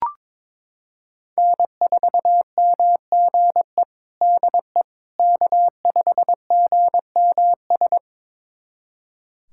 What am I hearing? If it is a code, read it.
Morse code, heard as a keyed tone, about 21 wpm, 700 Hz. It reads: N4MGE DE K5GMS
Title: USA Callsign DE Callsign - 5 Characters - 22wpm